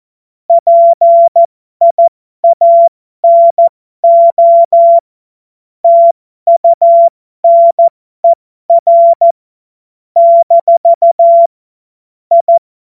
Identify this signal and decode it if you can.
Morse code, heard as a keyed tone, about 14 wpm, spaced as for 12 wpm, 680 Hz. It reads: PIANO TUNER - I
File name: Morsekurs / Witze